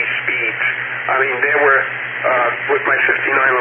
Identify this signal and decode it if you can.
Emergency Alert System (SAME) tones, heard as harsh buzzing bursts, with speech. It is heard from 0.6 to 3.2 s.